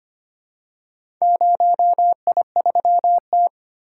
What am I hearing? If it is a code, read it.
Morse code, heard as a keyed tone, about 25 wpm, 700 Hz. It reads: 0I3T
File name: practice / 25wpm / Four Letter-Number Combinations - 25wpm